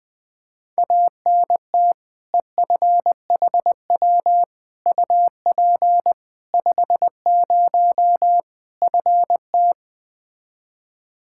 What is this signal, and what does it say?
Morse code, heard as a keyed tone, about 21 wpm, 700 Hz. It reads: ANT EFHW UP 50 FT
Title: Practice QSO - Exchange 2 - Info - 20wpm